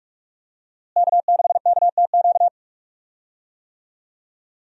Morse code, heard as a keyed tone, about 46 wpm, 700 Hz. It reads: K6KTQ
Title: USA Call Signs - 5 Characters - Repeated 3x - 45wpm